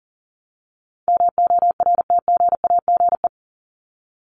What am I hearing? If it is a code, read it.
Morse code, heard as a keyed tone, about 39 wpm, 700 Hz. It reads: MORTGAGE
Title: Top 1751-2000 Words 40wpm